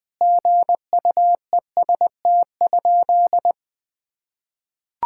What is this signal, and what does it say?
Morse code, heard as a keyed tone, about 19 wpm, 700 Hz. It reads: GUEST?
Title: Sentences from Top 1600 Words 20wpm